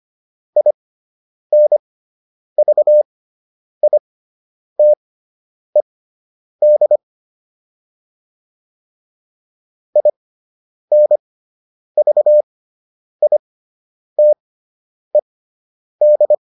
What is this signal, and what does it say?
Morse code, heard as a keyed tone, about 25 wpm, spaced as for 9 wpm, 600 Hz. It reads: INVITED INVITED